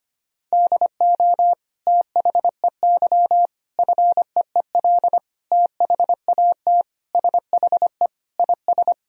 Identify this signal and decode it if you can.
Morse code, heard as a keyed tone, about 25 wpm, 700 Hz. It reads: DO THEY FEEL THAT SHE IS
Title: Sentences from Top 1000 Words - Review Part 1 - 25wpm